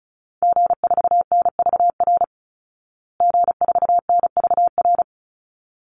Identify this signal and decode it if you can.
Morse code, heard as a keyed tone, about 34 wpm, 700 Hz. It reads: G4NVR G4NVR
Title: Call Signs Repeated 3x - 35wpm